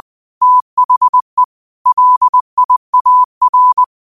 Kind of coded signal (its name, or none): Morse code